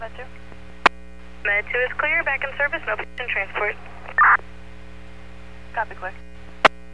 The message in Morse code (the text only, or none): none